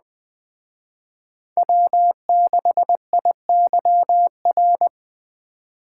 Morse code, heard as a keyed tone, about 20 wpm, 700 Hz. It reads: W6IYR
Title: Speed Racing - USA 1x3 Call Signs 17wpm